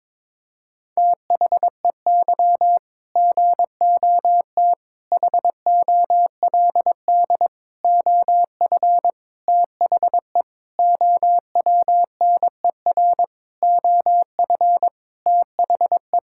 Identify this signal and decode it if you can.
Morse code, heard as a keyed tone, about 21 wpm, 700 Hz. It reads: THEY GOT HOLD OF THE OWNER OF THE
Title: Sentences from Top 1600 Words 22wpm